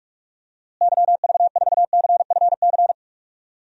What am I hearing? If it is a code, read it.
Morse code, heard as a keyed tone, about 46 wpm, 700 Hz. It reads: YV4CFC
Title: DXCC-100 Call Signs - 6 Characters - Repeated 2x - 45wpm